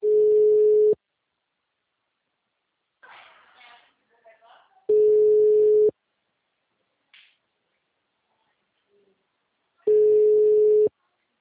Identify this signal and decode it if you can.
Ringback tone (Europe, 425 Hz), with speech.